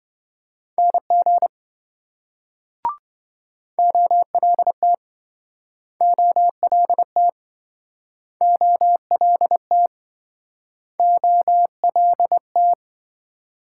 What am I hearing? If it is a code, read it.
Morse code, heard as a keyed tone, about 28 wpm, 700 Hz. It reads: NG OLT OLT OLT OLT